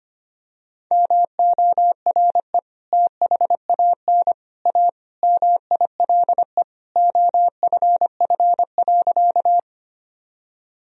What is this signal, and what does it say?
Morse code, heard as a keyed tone, about 25 wpm, 700 Hz. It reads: MORE THAN A MILE OFF.
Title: Sentences from Top 1000 Words - Review Part 1 - 25wpm